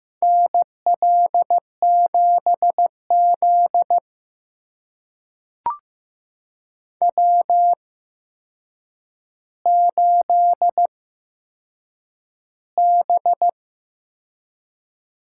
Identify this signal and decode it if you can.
Morse code, heard as a keyed tone, about 15 wpm, spaced as for 7 wpm, 700 Hz. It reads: NL7Z W8B
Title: USA Call Signs - 4 Characters - 8x Character Spacing - 15wpm